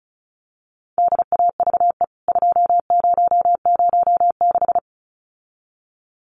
Morse code, heard as a keyed tone, about 34 wpm, 700 Hz. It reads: DAVE 2006
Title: CWT Exchange 35wpm